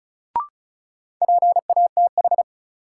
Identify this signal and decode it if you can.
Morse code, heard as a keyed tone, about 36 wpm, 700 Hz. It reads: PATH
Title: Sentences from Top 1400 Words - New Word Drill Alt - 35wpm